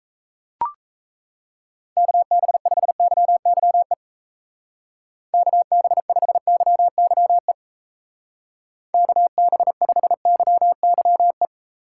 Morse code, heard as a keyed tone, about 35 wpm, 700 Hz. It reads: KB5YYE KB5YYE KB5YYE
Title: Speed Racing - USA Call Signs 28wpm